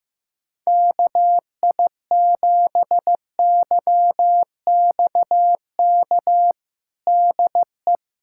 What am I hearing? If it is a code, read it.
Morse code, heard as a keyed tone, about 15 wpm, 700 Hz. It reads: KI7YXK DE
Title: USA Callsign DE Callsign 15wpm